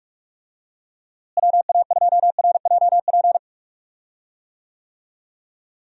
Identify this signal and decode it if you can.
Morse code, heard as a keyed tone, about 46 wpm, 700 Hz. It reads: WA2RJP
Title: USA Call Signs - 6 Characters - Repeated 2x - 45wpm